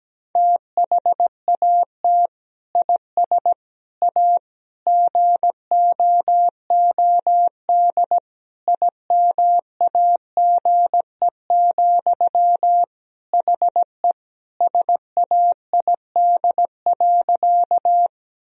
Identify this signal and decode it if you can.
Morse code, heard as a keyed tone, about 17 wpm, 700 Hz. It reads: THAT IS A GOOD IMAGE, HE SAID.
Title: Sentences from Top 1000 Words - Review Part 4 - 17wpm